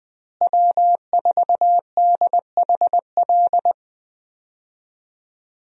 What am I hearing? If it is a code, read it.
Morse code, heard as a keyed tone, about 20 wpm, 700 Hz. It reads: W4DHL
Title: USA Call Signs - 5 Characters - Repeated 2x - 20wpm